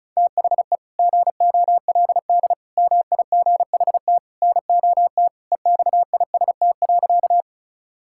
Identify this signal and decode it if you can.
Morse code, heard as a keyed tone, about 34 wpm, 690 Hz. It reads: THE GOLD MIGHT NOT EXIST.
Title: Sentences from Top 1000 Words - Review Part 5 - 35wpm